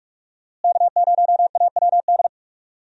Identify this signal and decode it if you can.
Morse code, heard as a keyed tone, about 42 wpm, 700 Hz. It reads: K0AWD
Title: USA Call Signs - 5 Characters - Repeated 2x - 45wpm